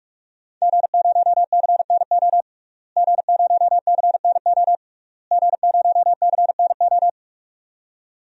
Morse code, heard as a keyed tone, about 46 wpm, 700 Hz. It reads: G0CNO G0CNO G0CNO
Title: DXCC-100 Call Signs - 5 Characters - Repeated 3x - 45wpm